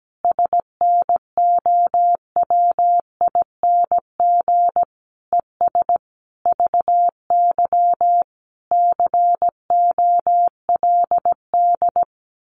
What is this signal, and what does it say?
Morse code, heard as a keyed tone, about 17 wpm, 700 Hz. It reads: SNOWING ES VY COLD
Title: Common QSO Phrases 17wpm